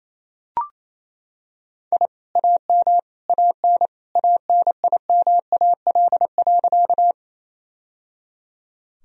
Morse code, heard as a keyed tone, about 28 wpm, 700 Hz. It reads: I AM AN ANIMAL.
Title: Sentences from Top 1000 Words - Review Part 1 - 28wpm